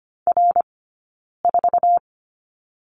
Morse code, heard as a keyed tone, about 25 wpm, spaced as for 9 wpm, 690 Hz. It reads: R4
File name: course / 25:10wpm / 092 Lesson 16d - Letter c - Two Characters 25wpm